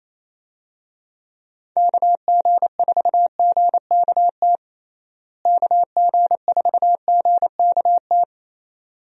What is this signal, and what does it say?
Morse code, heard as a keyed tone, about 28 wpm, 700 Hz. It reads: KG4GKT KG4GKT
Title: USA Call Signs - 6 Characters - Repeated 2x - 28wpm